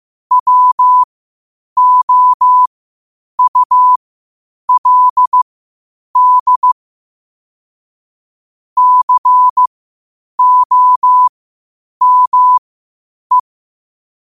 Morse code, heard as a keyed tone, about 15 wpm, spaced as for 8 wpm, 1000 Hz. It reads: WOULD COME
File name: qotdfiles / 8wpm